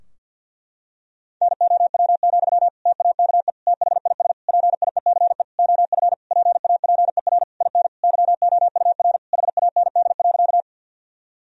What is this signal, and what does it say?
Morse code, heard as a keyed tone, about 50 wpm, 700 Hz. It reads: NOW, TAKE THIS PIECE OF PAPER IN YOUR HAND.
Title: Sentences from Top 1000 Words - Review Part 2 - 50wpm